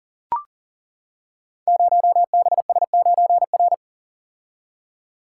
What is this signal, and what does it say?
Morse code, heard as a keyed tone, about 40 wpm, 700 Hz. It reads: 0BS9R
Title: Code Groups - 5 Letter:Numbers - 40wpm